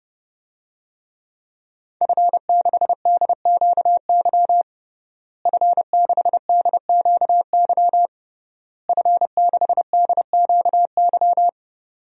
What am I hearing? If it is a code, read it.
Morse code, heard as a keyed tone, about 30 wpm, 700 Hz. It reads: F6DQY F6DQY F6DQY
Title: DXCC-100 Call Signs - 5 Characters - Repeated 3x - 30wpm